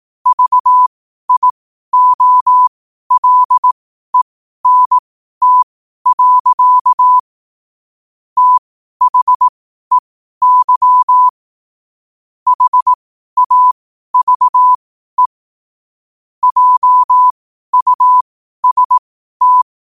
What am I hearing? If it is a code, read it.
Morse code, heard as a keyed tone, about 18 wpm, spaced as for 12 wpm, 1000 Hz. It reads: VIOLENT. THEY HAVE JUST
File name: qotdfiles / 12wpm